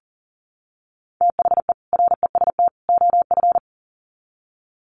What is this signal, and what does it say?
Morse code, heard as a keyed tone, about 40 wpm, 700 Hz. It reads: THE REST OF
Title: Sets of 3 Words 40wpm